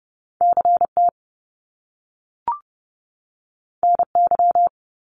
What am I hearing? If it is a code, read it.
Morse code, heard as a keyed tone, about 30 wpm, 700 Hz. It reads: CT NY
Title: Top Two Letter Combinations in Words 30wpm